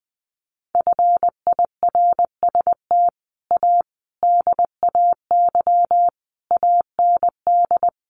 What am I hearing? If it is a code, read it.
Morse code, heard as a keyed tone, about 20 wpm, 700 Hz. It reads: FIRST A DAY AND
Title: A Sound Of Thunder 20wpm